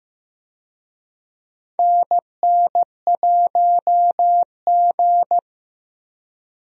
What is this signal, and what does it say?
Morse code, heard as a keyed tone, about 15 wpm, 700 Hz. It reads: NN1G
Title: USA Call Signs - 4 Characters - Repeated 3x - 15wpm